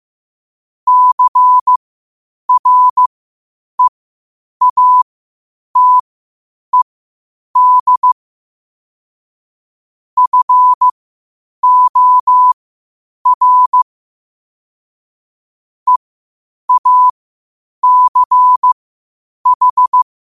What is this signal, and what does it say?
Morse code, heard as a keyed tone, about 15 wpm, spaced as for 8 wpm, 1000 Hz. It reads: CREATED FOR EACH
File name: qotdfiles / 8wpm